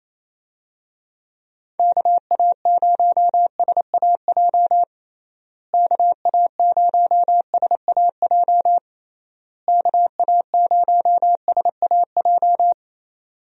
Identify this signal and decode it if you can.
Morse code, heard as a keyed tone, about 28 wpm, 700 Hz. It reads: KA0SAJ KA0SAJ KA0SAJ
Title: USA Call Signs - 6 Characters - Repeated 3x - 28wpm